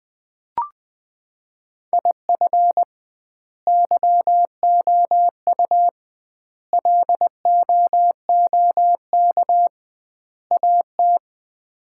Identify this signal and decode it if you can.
Morse code, heard as a keyed tone, about 20 wpm, 700 Hz. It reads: IF YOU LOOK AT
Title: 2x Word-Spacing - Sets of 4 Words - 20wpm